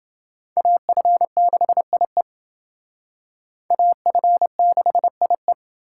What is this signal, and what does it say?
Morse code, heard as a keyed tone, about 28 wpm, 700 Hz. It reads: AF6IE AF6IE